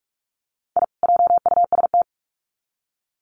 Morse code, heard as a keyed tone, about 44 wpm, 700 Hz. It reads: I JUST
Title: Sets of 2 Words 45wpm